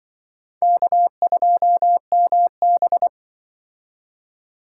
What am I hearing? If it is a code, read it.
Morse code, heard as a keyed tone, about 24 wpm, 700 Hz. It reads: K2MB
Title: Speed Racing - USA 1x2 Call Signs 20wpm